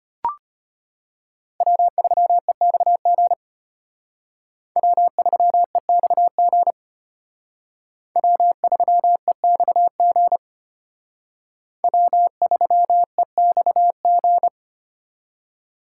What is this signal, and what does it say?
Morse code, heard as a keyed tone, about 32 wpm, 700 Hz. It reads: W3EXG W3EXG W3EXG W3EXG